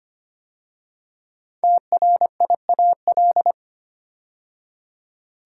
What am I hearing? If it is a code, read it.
Morse code, heard as a keyed tone, about 25 wpm, 700 Hz. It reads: TRIAL
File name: 5 Letter Words - First 3 Letters Given - 25wpm